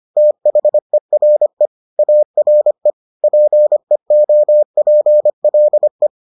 Morse code, heard as a keyed tone, about 25 wpm, 600 Hz. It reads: THERE ARE PEOPLE